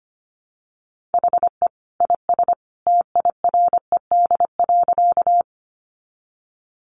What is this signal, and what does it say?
Morse code, heard as a keyed tone, about 25 wpm, 700 Hz. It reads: HE IS TIRED.
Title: Additional Easy Phrases 25wpm